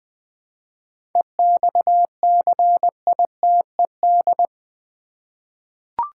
Morse code, heard as a keyed tone, about 20 wpm, 700 Hz. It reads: EXCITED
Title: 7 Letter Words - First 3 Letters Given - 20wpm